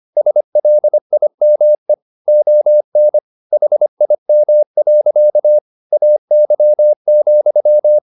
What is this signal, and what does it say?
Morse code, heard as a keyed tone, about 25 wpm, 600 Hz. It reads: SLIME ON HIM. AY,